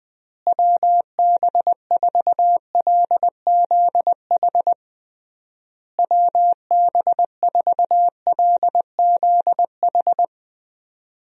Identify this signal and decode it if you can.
Morse code, heard as a keyed tone, about 20 wpm, 700 Hz. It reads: WB4LZH WB4LZH
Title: USA Call Signs - 6 Characters - Repeated 3x - 20wpm